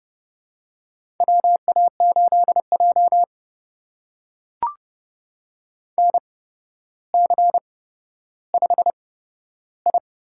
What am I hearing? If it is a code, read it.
Morse code, heard as a keyed tone, about 30 wpm, spaced as for 13 wpm, 700 Hz. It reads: WA8J NC5I
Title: USA Call Signs - 4 Characters - 8x Character Spacing - 30wpm